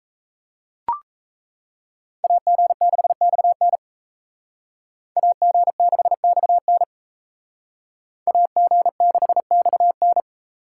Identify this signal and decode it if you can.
Morse code, heard as a keyed tone, about 36 wpm, 700 Hz. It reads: AG6XN AG6XN AG6XN